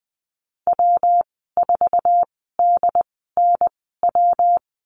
Morse code, heard as a keyed tone, about 20 wpm, spaced as for 15 wpm, 700 Hz. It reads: W4DNW
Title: 179 Lesson 29g - Slash - Call Signs 20wpm